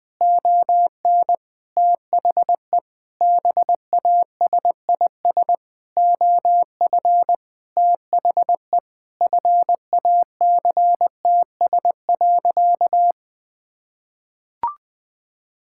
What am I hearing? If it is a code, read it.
Morse code, heard as a keyed tone, about 20 wpm, 700 Hz. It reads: ON THE BASIS OF THE FACTS.
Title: Sentences with 5-Word Phrases - 20wpm